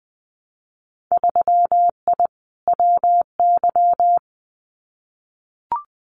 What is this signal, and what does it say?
Morse code, heard as a keyed tone, about 20 wpm, 700 Hz. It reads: 3I WY
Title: Winter Field Day Exchanges Encore 20wpm